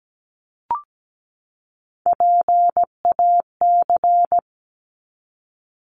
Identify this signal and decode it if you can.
Morse code, heard as a keyed tone, about 17 wpm, 700 Hz. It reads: PAC
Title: ARRL Sections 17wpm